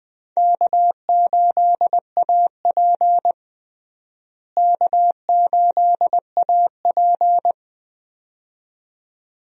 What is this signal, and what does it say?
Morse code, heard as a keyed tone, about 20 wpm, 700 Hz. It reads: K8AP K8AP
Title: USA Call Signs - 4 Characters - Repeated 2x - 20wpm